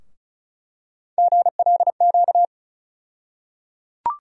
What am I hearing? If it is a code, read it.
Morse code, heard as a keyed tone, about 36 wpm, 700 Hz. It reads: GLQ
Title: USA Call Sign Postfixes - 3 Letters 35wpm